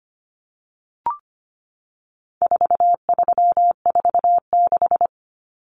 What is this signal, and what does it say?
Morse code, heard as a keyed tone, about 25 wpm, 700 Hz. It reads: 4346
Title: Numbers - 4-Digit 25wpm